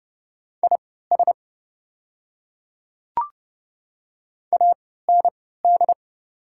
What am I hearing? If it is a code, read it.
Morse code, heard as a keyed tone, about 30 wpm, spaced as for 17 wpm, 700 Hz. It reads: IS AND
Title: 3x Character Spacing - Top 100 Words - 30wpm